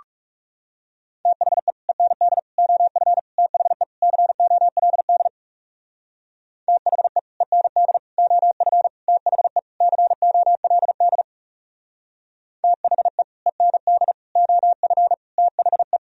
Morse code, heard as a keyed tone, about 39 wpm, 700 Hz. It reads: THE END OF THE COLD THE END OF THE COLD THE END OF THE